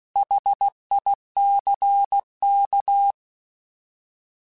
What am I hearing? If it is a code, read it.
Morse code, heard as a keyed tone, about 16 wpm, 800 Hz. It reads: HICK
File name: Alphabetical / Alphabetical mp3s / Hs @15